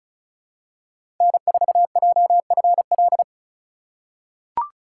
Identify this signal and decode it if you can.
Morse code, heard as a keyed tone, about 36 wpm, 700 Hz. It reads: N4JFL
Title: USA Call Signs - 5 Characters - 2x Character Spacing - 35wpm